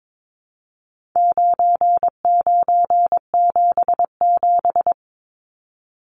Morse code, heard as a keyed tone, about 22 wpm, 700 Hz. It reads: 9977
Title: Numbers - 4-Digit 22wpm